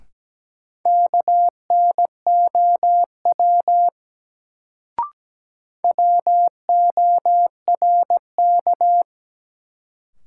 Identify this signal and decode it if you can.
Morse code, heard as a keyed tone, about 17 wpm, 700 Hz. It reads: KNOW WORK
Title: Sets of 2 Words - New Word Drill - 17wpm